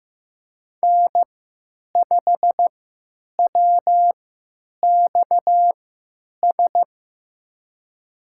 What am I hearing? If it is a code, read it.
Morse code, heard as a keyed tone, about 15 wpm, spaced as for 9 wpm, 700 Hz. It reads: N5WXS
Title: USA Call Signs - 5 Characters - 3x Character Spacing - 15wpm